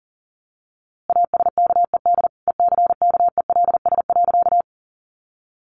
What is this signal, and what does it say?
Morse code, heard as a keyed tone, about 40 wpm, 700 Hz. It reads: ASKED ECKELS.
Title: A Sound Of Thunder 40wpm